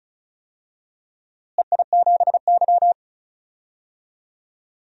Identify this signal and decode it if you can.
Morse code, heard as a keyed tone, about 36 wpm, 700 Hz. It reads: EI7Y
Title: DXCC-100 Call Signs - 4 Characters - Repeated 2x - 35wpm